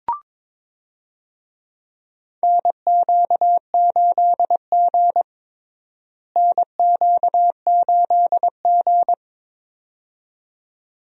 Morse code, heard as a keyed tone, about 22 wpm, 700 Hz. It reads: NQ8G NQ8G
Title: USA Call Signs - 4 Characters - Repeated 2x - 22wpm